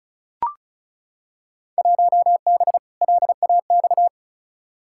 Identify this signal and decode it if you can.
Morse code, heard as a keyed tone, about 36 wpm, 700 Hz. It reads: 1B LAX
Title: Field Day Exchanges - 35wpm